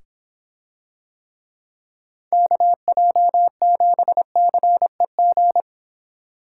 Morse code, heard as a keyed tone, about 26 wpm, 700 Hz. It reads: KJ7CEG